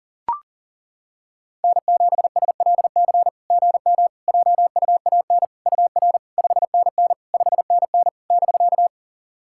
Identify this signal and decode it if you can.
Morse code, heard as a keyed tone, about 40 wpm, 690 Hz. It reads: N7SLC GM JUAN UR 5NN 5NN <BK>
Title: SOTA QSO Practice 40wpm